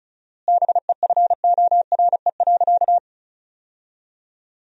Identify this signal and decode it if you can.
Morse code, heard as a keyed tone, about 36 wpm, 700 Hz. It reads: BEFORE.
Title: 4x Word-Spacing - Sentences from Top 400 Words - 35wpm